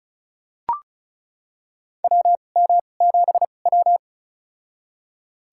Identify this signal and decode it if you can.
Morse code, heard as a keyed tone, about 36 wpm, spaced as for 25 wpm, 700 Hz. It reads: WM7W
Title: USA Call Signs - 4 Characters - 2x Character Spacing - 35wpm